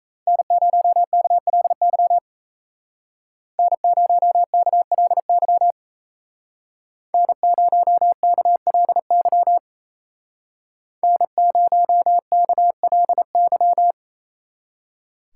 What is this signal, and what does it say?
Morse code, heard as a keyed tone, about 36 wpm, 700 Hz. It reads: N0KLY N0KLY N0KLY N0KLY